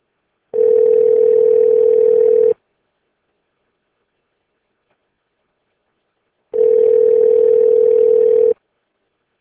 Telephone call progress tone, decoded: ringback tone